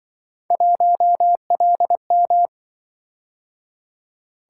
Morse code, heard as a keyed tone, about 25 wpm, 700 Hz. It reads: 1LM
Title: Speed-Racing - Three Letter-Number Combinations - 20wpm